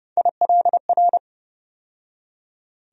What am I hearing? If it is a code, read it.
Morse code, heard as a keyed tone, about 30 wpm, 700 Hz. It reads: ILR
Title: USA Call Sign Postfixes - 3 Letters 30wpm